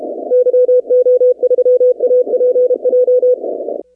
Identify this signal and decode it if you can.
Morse code, heard as a keyed tone, about 32 wpm, 500 Hz. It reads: YO3APJ
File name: 2019-cw-wpx